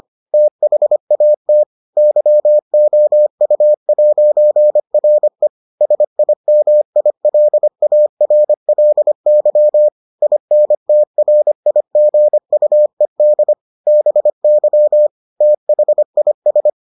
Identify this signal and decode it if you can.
Morse code, heard as a keyed tone, about 25 wpm, 600 Hz. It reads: THAT YOU'RE SIMILARLY INTRIGUED BY THIS